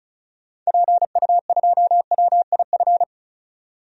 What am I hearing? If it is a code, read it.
Morse code, heard as a keyed tone, about 34 wpm, 700 Hz. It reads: PU2WIF
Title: DXCC-100 Call Signs - 6 Characters - 35wpm